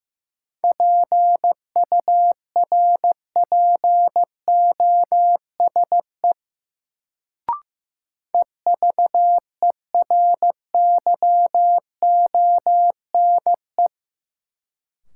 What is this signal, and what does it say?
Morse code, heard as a keyed tone, about 15 wpm, 700 Hz. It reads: PURPOSE EVERYONE
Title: Sentences from Top 800 Words - New Word Drill - 15wpm